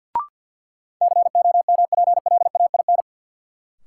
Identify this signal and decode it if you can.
Morse code, heard as a keyed tone, about 49 wpm, 700 Hz. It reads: COMPLAIN
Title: Sentences from Top 1300 Words - New Word Drill Alt - 50wpm